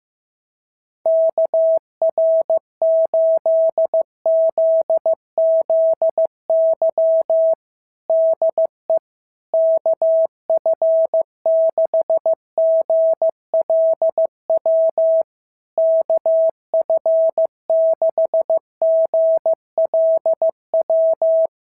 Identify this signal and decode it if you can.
Morse code, heard as a keyed tone, about 15 wpm, 650 Hz. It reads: KR8ZZY DE KF6GLW KF6GLW
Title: SOTA QSO Practice 15wpm